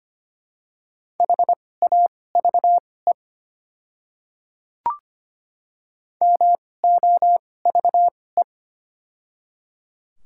Morse code, heard as a keyed tone, about 25 wpm, spaced as for 18 wpm, 700 Hz. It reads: HAVE MOVE
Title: 2x Character Spacing - Top 200 Words - 25wpm